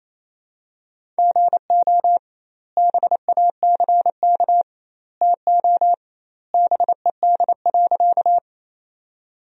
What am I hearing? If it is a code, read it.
Morse code, heard as a keyed tone, about 28 wpm, 700 Hz. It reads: GO BACK TO BED.
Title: 2x Word-Spacing - Sentences from Top 500 Words - 28wpm